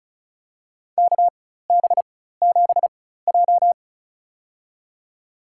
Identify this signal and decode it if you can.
Morse code, heard as a keyed tone, about 36 wpm, spaced as for 16 wpm, 700 Hz. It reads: KB7J
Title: USA Call Signs - 4x Character Spacing - 35wpm